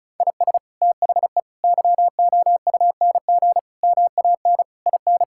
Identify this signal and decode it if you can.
Morse code, heard as a keyed tone, about 34 wpm, 700 Hz. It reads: IS THE YOUNG MAN IN
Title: Sentences from Top 1600 Words 35wpm